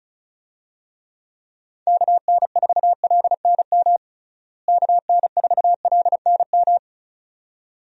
Morse code, heard as a keyed tone, about 36 wpm, 700 Hz. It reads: KN4LNM KN4LNM
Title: USA Call Signs - 6 Characters - Repeated 2x - 35wpm